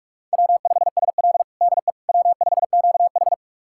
Morse code, heard as a keyed tone, about 44 wpm, 690 Hz. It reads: W5SL DE W5QH
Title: USA Callsign DE Callsign - 4 Characters - 45wpm